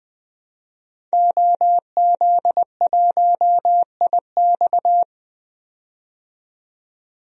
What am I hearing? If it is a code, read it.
Morse code, heard as a keyed tone, about 20 wpm, 700 Hz. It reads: OZ1IX
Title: DXCC-100 Call Signs - 5 Characters - 20wpm